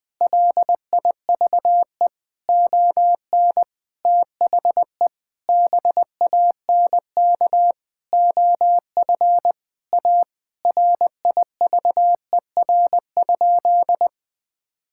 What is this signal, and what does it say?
Morse code, heard as a keyed tone, about 20 wpm, 700 Hz. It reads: LIVE ON THE BANK OF A RIVER?
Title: Sentences from Top 1000 Words - Review Part 6 - 20wpm